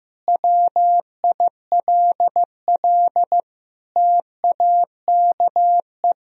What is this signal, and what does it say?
Morse code, heard as a keyed tone, about 15 wpm, 700 Hz. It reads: WILL TAKE
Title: Sentences from Top 1000 Words - Review Part 3 - 15wpm